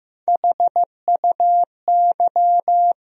Morse code, heard as a keyed tone, about 15 wpm, 700 Hz. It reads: HUY
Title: USA Call Sign Postfixes - 3 Letters 15wpm